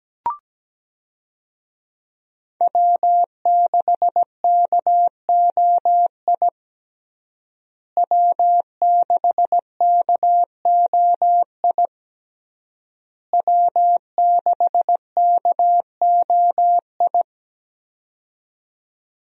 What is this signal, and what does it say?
Morse code, heard as a keyed tone, about 17 wpm, 700 Hz. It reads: W6KOI W6KOI W6KOI
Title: USA Call Signs - 5 Characters - Repeated 3x - 17wpm